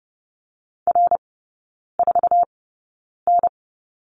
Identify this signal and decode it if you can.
Morse code, heard as a keyed tone, about 30 wpm, spaced as for 9 wpm, 700 Hz. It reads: R4N
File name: Lesson 18e - Letter m - Three Characters 30wpm